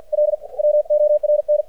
Morse code, heard as a keyed tone, about 50 wpm, 600 Hz. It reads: L WORN
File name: archived / CRSnet